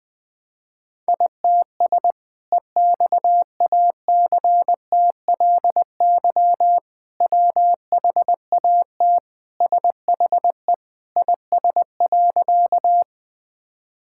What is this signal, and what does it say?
Morse code, heard as a keyed tone, about 21 wpm, 700 Hz. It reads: ITS EXACTLY WHAT SHE IS.
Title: Sentences from Top 1000 Words - Review Part 6 - 20wpm